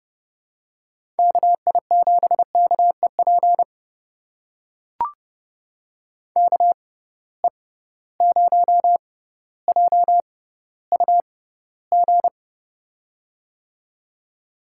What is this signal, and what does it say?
Morse code, heard as a keyed tone, about 30 wpm, spaced as for 15 wpm, 700 Hz. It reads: KI7KEP KE0JUG
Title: USA Call Signs - 6 Characters - 6x Character Spacing - 30wpm